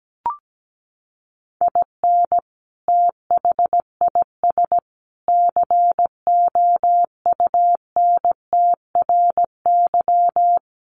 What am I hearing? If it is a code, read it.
Morse code, heard as a keyed tone, about 17 wpm, 700 Hz. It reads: IN THIS COUNTRY
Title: Sets of 3 Words 17wpm